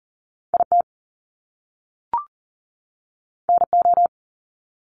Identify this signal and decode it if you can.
Morse code, heard as a keyed tone, about 40 wpm, 700 Hz. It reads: IT NO
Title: Top Two Letter Combinations in Words 40wpm